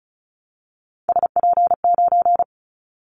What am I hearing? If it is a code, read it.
Morse code, heard as a keyed tone, about 36 wpm, 690 Hz. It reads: SP9
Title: Callsign Prefixes for Top 25 DXCC - 35wpm